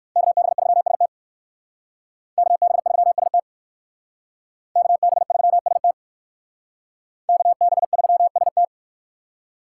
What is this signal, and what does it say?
Morse code, heard as a keyed tone, about 58 wpm, 700 Hz. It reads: KB3ST KB3ST KB3ST KB3ST